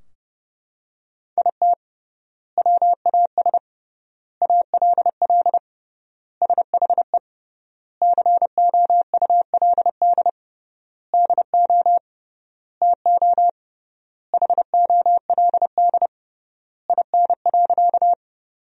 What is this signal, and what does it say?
Morse code, heard as a keyed tone, about 30 wpm, 700 Hz. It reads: IT WAS ALL SHE COULD DO TO HOLD IN.
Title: 3x Word-Spacing - Sentences from Top 400 Words - 30wpm